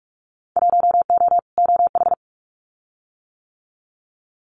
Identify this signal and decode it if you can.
Morse code, heard as a keyed tone, about 46 wpm, 700 Hz. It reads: 1O OH
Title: Winter Field Day Exchanges 45wpm